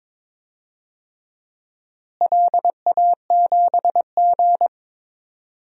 Morse code, heard as a keyed tone, about 21 wpm, 700 Hz. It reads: LA7G
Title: DXCC-100 Call Signs - 4 Characters - Repeated 3x - 22wpm